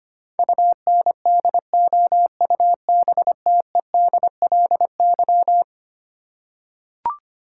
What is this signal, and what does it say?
Morse code, heard as a keyed tone, about 25 wpm, 700 Hz. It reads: UNDOUBTEDLY
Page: morse-code-ninja-practice-25wpm